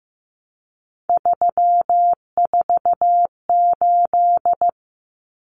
Morse code, heard as a keyed tone, about 14 wpm, 700 Hz. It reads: 348
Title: Common Cut Numbers - 3-Digit 15wpm